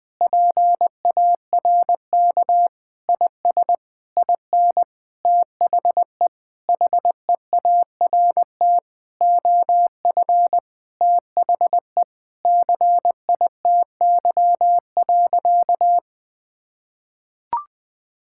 Morse code, heard as a keyed tone, about 20 wpm, 700 Hz. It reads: PARK IS IN THE HEART OF THE CITY.
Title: Sentences with 4-Word Phrases - 20wpm